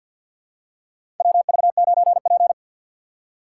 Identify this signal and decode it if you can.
Morse code, heard as a keyed tone, about 52 wpm, 700 Hz. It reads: WV9P
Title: USA Call Signs - 4 Characters - 6x Character Spacing - 50wpm